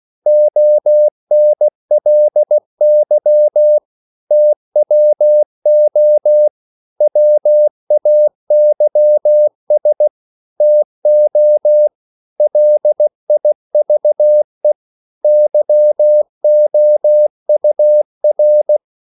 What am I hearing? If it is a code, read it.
Morse code, heard as a keyed tone, about 16 wpm, 600 Hz. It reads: ONLY TWO WAYS TO LIVE YOUR
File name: CW / Practice / EnglishQuotations0416